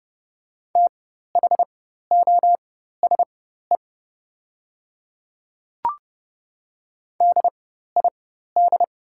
Morse code, heard as a keyed tone, about 31 wpm, spaced as for 14 wpm, 700 Hz. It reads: THOSE DID
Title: 4x Character Spacing - Top 300 Words - 30wpm